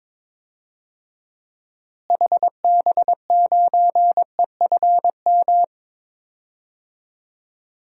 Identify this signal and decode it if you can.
Morse code, heard as a keyed tone, about 21 wpm, 700 Hz. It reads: HB9EFM
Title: DXCC-100 Call Signs - 6 Characters - Repeated 2x - 22wpm